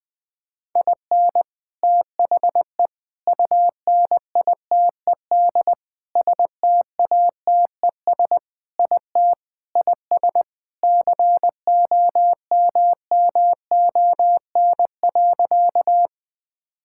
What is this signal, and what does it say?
Morse code, heard as a keyed tone, about 20 wpm, 700 Hz. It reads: IN THE UNITED STATES IT IS COMMON.
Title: Sentences with 4-Word Phrases - 20wpm